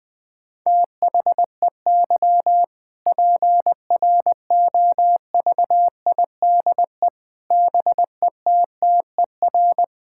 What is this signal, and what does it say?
Morse code, heard as a keyed tone, about 20 wpm, 700 Hz. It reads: THEY PROVIDE BETTER
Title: Sentences from Top 1000 Words - Review Part 5 - 20wpm